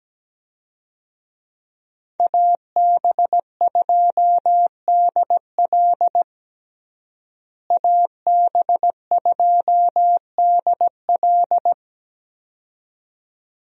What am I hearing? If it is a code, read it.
Morse code, heard as a keyed tone, about 17 wpm, 700 Hz. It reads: AB2DL AB2DL
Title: USA Call Signs - 5 Characters - Repeated 2x - 17wpm